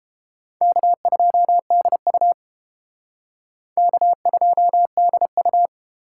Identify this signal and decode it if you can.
Morse code, heard as a keyed tone, about 31 wpm, 700 Hz. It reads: K2DU K2DU